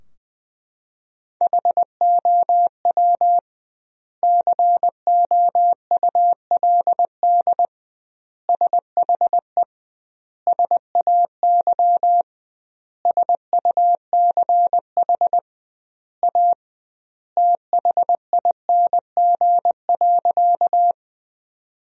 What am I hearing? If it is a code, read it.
Morse code, heard as a keyed tone, about 21 wpm, 700 Hz. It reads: HOW COULD SHE SAY SUCH A THING.
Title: 2x Word-Spacing - Sentences from Top 200 Words - 20wpm